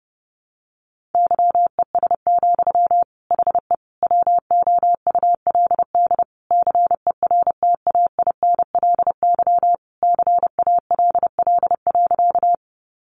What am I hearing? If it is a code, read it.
Morse code, heard as a keyed tone, about 30 wpm, 700 Hz. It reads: YES, HE WOULD CERTAINLY CALL.
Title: Sentences from Top 900 Words 30wpm